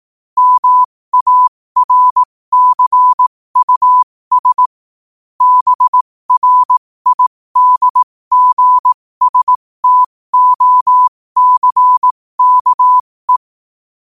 Morse code, heard as a keyed tone, about 18 wpm, spaced as for 15 wpm, 1000 Hz. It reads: MARCUS BRIDGSTOCKE
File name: qotdfiles / 15wpm